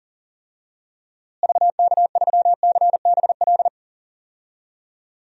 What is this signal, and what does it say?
Morse code, heard as a keyed tone, about 37 wpm, 700 Hz. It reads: VK3CBL
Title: DXCC-100 Call Signs - 6 Characters - Repeated 2x - 40wpm